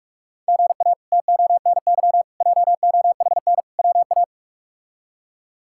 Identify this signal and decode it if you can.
Morse code, heard as a keyed tone, about 41 wpm, 700 Hz. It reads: GA TONY JOHN WA
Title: K1USN SST QSO Practice - 45wpm